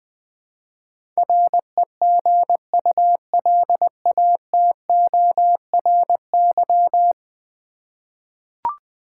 Morse code, heard as a keyed tone, about 21 wpm, 700 Hz. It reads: REGULATORY
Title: Ten Letter Words - 20wpm